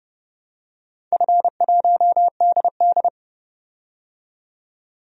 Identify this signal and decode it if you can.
Morse code, heard as a keyed tone, about 31 wpm, 700 Hz. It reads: F1DD
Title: DXCC-100 Call Signs - 4 Characters - 30wpm